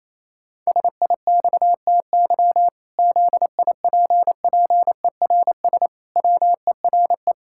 Morse code, heard as a keyed tone, about 28 wpm, 700 Hz. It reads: SIXTY ZIPPERS WERE
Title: Pangram Challenge v2 - 28wpm